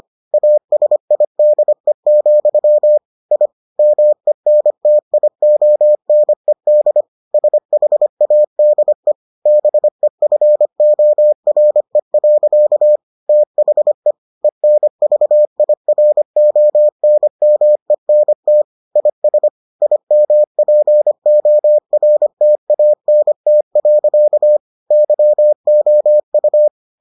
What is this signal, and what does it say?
Morse code, heard as a keyed tone, about 25 wpm, 600 Hz. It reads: ASIDE, I MENTIONED SHADE BEFORE. THE ENVIRONMENT IS IMPORTANT. YOU